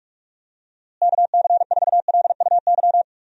Morse code, heard as a keyed tone, about 46 wpm, 700 Hz. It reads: KC4LUY
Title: USA Call Signs - 6 Characters - Repeated 3x - 45wpm